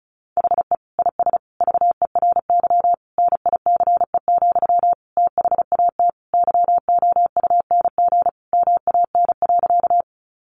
Morse code, heard as a keyed tone, about 34 wpm, 700 Hz. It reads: HE IS VERY NICE, THAT YOUNG MAN.
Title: Sentences from Top 1100 Words 35wpm